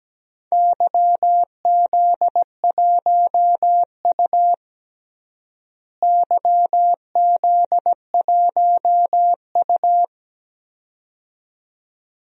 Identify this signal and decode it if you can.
Morse code, heard as a keyed tone, about 17 wpm, 700 Hz. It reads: YZ1U YZ1U
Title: DXCC-100 Call Signs - 4 Characters - Repeated 2x - 17wpm